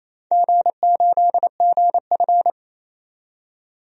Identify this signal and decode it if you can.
Morse code, heard as a keyed tone, about 28 wpm, 700 Hz. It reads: G8GF
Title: DXCC-100 Call Signs - 4 Characters - 28wpm